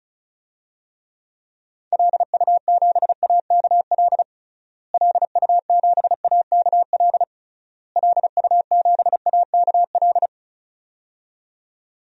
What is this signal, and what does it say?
Morse code, heard as a keyed tone, about 34 wpm, 700 Hz. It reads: LU7AKL LU7AKL LU7AKL
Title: DXCC-100 Call Signs - 6 Characters - Repeated 3x - 35wpm